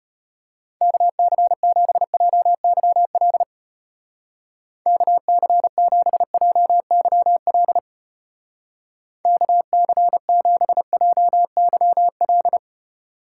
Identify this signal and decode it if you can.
Morse code, heard as a keyed tone, about 31 wpm, 700 Hz. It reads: KC7JYL KC7JYL KC7JYL
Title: Speed Racing - USA Call Signs 25wpm